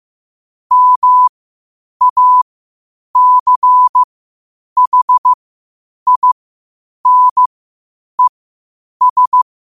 Morse code, heard as a keyed tone, about 15 wpm, spaced as for 8 wpm, 1000 Hz. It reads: MACHINES
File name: qotdfiles / 8wpm